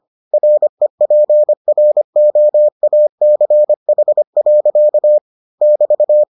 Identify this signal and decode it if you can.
Morse code, heard as a keyed tone, about 25 wpm, 600 Hz. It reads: REPROACH. <BT>